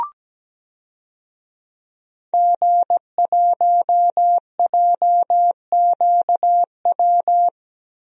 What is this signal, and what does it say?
Morse code, heard as a keyed tone, about 17 wpm, 700 Hz. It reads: G1JQW
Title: DXCC-100 Call Signs - 5 Characters - Repeated 3x - 17wpm